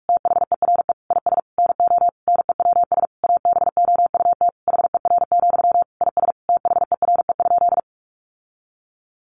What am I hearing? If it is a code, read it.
Morse code, heard as a keyed tone, about 45 wpm, 700 Hz. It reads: THERE IS NO NEWS ABOUT HER, IS THERE?
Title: Sentences from Top 1500 Words 45wpm